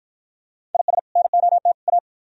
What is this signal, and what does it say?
Morse code, heard as a keyed tone, about 53 wpm, 700 Hz. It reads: IS NOT A